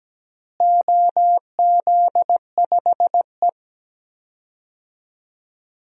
Morse code, heard as a keyed tone, about 17 wpm, 700 Hz. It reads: OZ5E
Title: DXCC-100 Call Signs - 4 Characters - Repeated 3x - 17wpm